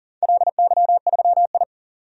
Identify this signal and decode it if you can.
Morse code, heard as a keyed tone, about 42 wpm, 700 Hz. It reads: LY3I
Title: DXCC-100 Call Signs - 4 Characters - 40wpm